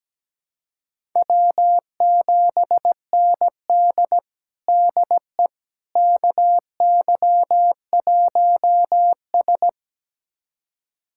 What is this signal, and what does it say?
Morse code, heard as a keyed tone, about 17 wpm, 700 Hz. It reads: W7ND DE KY1S
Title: USA Callsign DE Callsign - 4 Characters - 17wpm